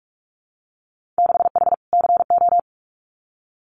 Morse code, heard as a keyed tone, about 46 wpm, 700 Hz. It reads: 6H CO
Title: Winter Field Day Exchanges Encore 45wpm